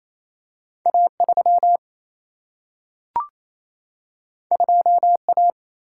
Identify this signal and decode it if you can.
Morse code, heard as a keyed tone, about 28 wpm, 700 Hz. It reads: A3 2A
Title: Single-Letter Single-Number 28wpm